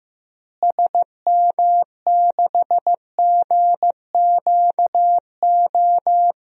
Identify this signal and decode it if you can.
Morse code, heard as a keyed tone, about 15 wpm, 700 Hz. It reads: SM6GQO
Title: DXCC-100 Call Signs - 6 Characters - Repeated 2x - 15wpm